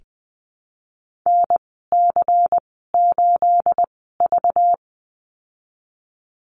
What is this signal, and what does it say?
Morse code, heard as a keyed tone, about 20 wpm, spaced as for 15 wpm, 700 Hz. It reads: NC8V
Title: 238 Lesson 38g - Letter q - Call Signs 20wpm